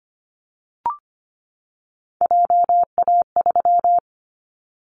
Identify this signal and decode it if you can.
Morse code, heard as a keyed tone, about 25 wpm, 700 Hz. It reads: JA3
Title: Callsign Prefixes for Top 25 DXCC - 25wpm